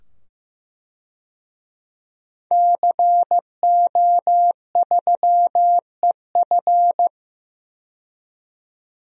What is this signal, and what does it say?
Morse code, heard as a keyed tone, about 15 wpm, 700 Hz. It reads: CO3EF
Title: DXCC-100 Call Signs - 5 Characters - 15wpm